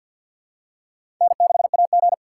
Morse code, heard as a keyed tone, about 51 wpm, 700 Hz. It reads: N6AG
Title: USA Call Signs - 4x Character Spacing - 50wpm